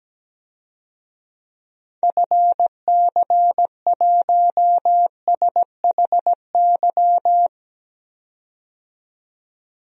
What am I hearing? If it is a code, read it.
Morse code, heard as a keyed tone, about 17 wpm, 700 Hz. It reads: FC1SHY
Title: DXCC-100 Call Signs - 6 Characters - 17wpm